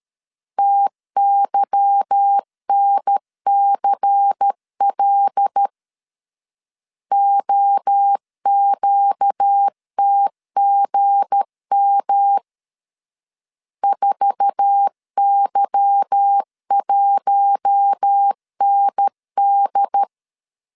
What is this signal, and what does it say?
Morse code, heard as a keyed tone, about 13 wpm, 800 Hz. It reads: TYNCL OQTGM 4Y1ND
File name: LEZIONI / 8 YJ45 / 8 YJ4 12wpm 2151